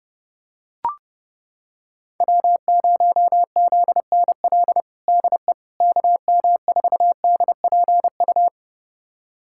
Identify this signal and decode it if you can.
Morse code, heard as a keyed tone, about 30 wpm, 700 Hz. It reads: W0ZNL DE KM4DPU
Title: USA Callsign DE Callsign 30wpm